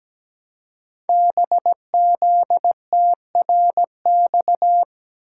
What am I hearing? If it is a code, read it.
Morse code, heard as a keyed tone, about 17 wpm, 700 Hz. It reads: BZTRX
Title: Code Groups - 5 Letter:Numbers - 17wpm